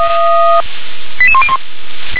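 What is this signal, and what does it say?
Five-tone selective call (ZVEI), heard as quick runs of five tones, tones of about 70 ms. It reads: 88101
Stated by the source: Five-tone ZVEI selective call 88666 received by SDR.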